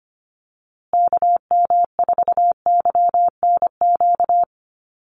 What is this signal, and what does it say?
Morse code, heard as a keyed tone, about 25 wpm, 700 Hz. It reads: KM4YNQ
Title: USA Call Signs Repeated 2x 25wpm